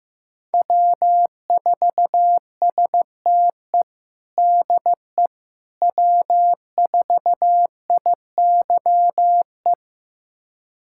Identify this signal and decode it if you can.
Morse code, heard as a keyed tone, about 15 wpm, 700 Hz. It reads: W4STE DE W4IYE
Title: USA Callsign DE Callsign - 5 Characters - 15wpm